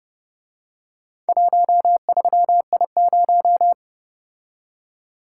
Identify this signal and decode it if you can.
Morse code, heard as a keyed tone, about 30 wpm, 700 Hz. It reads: 13I0